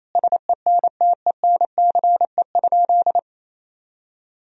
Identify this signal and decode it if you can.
Morse code, heard as a keyed tone, about 27 wpm, 700 Hz. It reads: SENTENCE?
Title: Sentences from Top 1000 Words - Review Part 1 - 28wpm